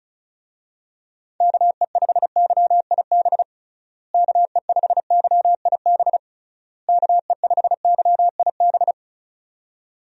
Morse code, heard as a keyed tone, about 36 wpm, 700 Hz. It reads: KE5YIB KE5YIB KE5YIB
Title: USA Call Signs - 6 Characters - Repeated 3x - 35wpm